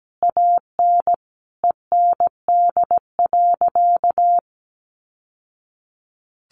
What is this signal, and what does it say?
Morse code, heard as a keyed tone, about 17 wpm, 700 Hz. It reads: AN END.
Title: Sentences from Top 1100 Words 17wpm